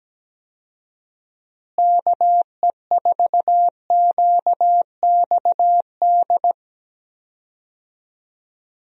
Morse code, heard as a keyed tone, about 17 wpm, 700 Hz. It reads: KE4QXD
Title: USA Call Signs - 6 Characters - Repeated 3x - 17wpm